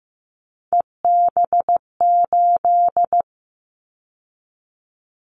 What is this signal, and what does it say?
Morse code, heard as a keyed tone, about 15 wpm, 700 Hz. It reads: EB8
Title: Callsign Prefixes for Top 100 DXCC - 15wpm